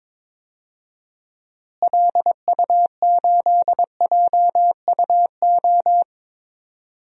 Morse code, heard as a keyed tone, about 22 wpm, 700 Hz. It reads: LU8JUO
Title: DXCC-100 Call Signs - 6 Characters - Repeated 3x - 22wpm